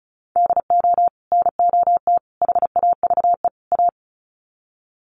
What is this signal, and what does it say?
Morse code, heard as a keyed tone, about 34 wpm, 700 Hz. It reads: DO NOT HAVE A
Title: Sets of 4 Words 35wpm